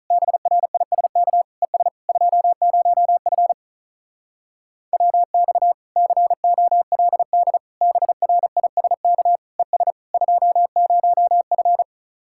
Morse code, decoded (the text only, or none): BRISK ES 20F WX COLD BRISK ES 20F